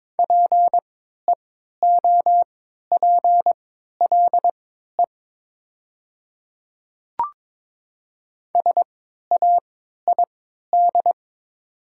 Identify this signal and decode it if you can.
Morse code, heard as a keyed tone, about 22 wpm, spaced as for 12 wpm, 700 Hz. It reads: PEOPLE SAID
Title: 3x Character Spacing - Top 100 Words - 22wpm